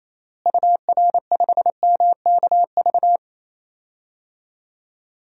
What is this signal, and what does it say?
Morse code, heard as a keyed tone, about 28 wpm, 700 Hz. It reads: UR5MKV
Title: DXCC-100 Call Signs - 6 Characters - Repeated 2x - 28wpm